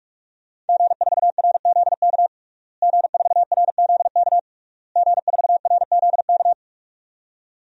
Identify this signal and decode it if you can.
Morse code, heard as a keyed tone, about 48 wpm, 700 Hz. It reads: G4RZK G4RZK G4RZK
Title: DXCC-100 Call Signs - 5 Characters - Repeated 3x - 45wpm